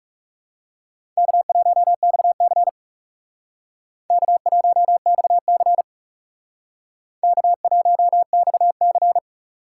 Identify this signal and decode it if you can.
Morse code, heard as a keyed tone, about 37 wpm, 700 Hz. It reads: K1XC K1XC K1XC
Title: Speed Racing - USA 1x2 Call Signs 30wpm